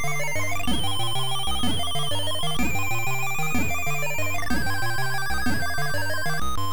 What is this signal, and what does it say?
There is no code.